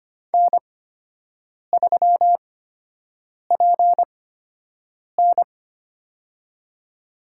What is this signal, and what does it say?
Morse code, heard as a keyed tone, about 25 wpm, spaced as for 7 wpm, 700 Hz. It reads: N3PN
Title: USA Call Signs - 4 Characters - 8x Character Spacing - 25wpm